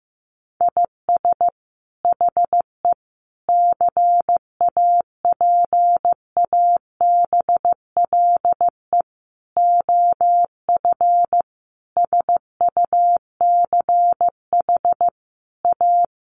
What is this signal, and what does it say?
Morse code, heard as a keyed tone, about 15 wpm, 700 Hz. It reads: IS HE CAPABLE OF SUCH A
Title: Sentences from Top 1200 Words 15wpm